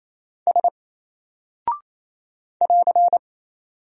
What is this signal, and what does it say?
Morse code, heard as a keyed tone, about 27 wpm, 700 Hz. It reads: S <AR>
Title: Single Character (Letter:Number:Punctuation:Prosign) 28wpm